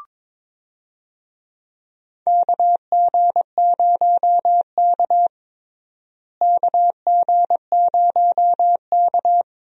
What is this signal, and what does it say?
Morse code, heard as a keyed tone, about 22 wpm, 700 Hz. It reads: KG0K KG0K
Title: USA Call Signs - 4 Characters - Repeated 2x - 22wpm